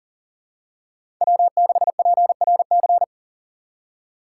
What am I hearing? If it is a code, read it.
Morse code, heard as a keyed tone, about 38 wpm, 700 Hz. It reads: W6PRC
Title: USA Call Signs - 5 Characters - 2x Character Spacing - 40wpm